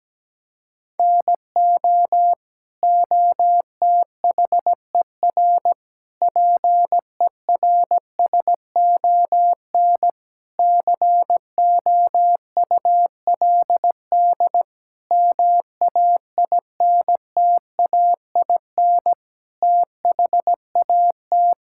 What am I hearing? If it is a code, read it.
Morse code, heard as a keyed tone, about 17 wpm, 700 Hz. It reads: NO OTHER PERSON COULD MAINTAIN THAT
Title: Sentences from Top 1000 Words - Review Part 5 - 17wpm